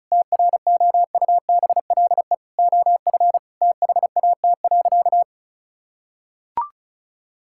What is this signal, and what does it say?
Morse code, heard as a keyed tone, about 36 wpm, 700 Hz. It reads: TROUBLE OF THAT.
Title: Sentences from Top 1600 Words 35wpm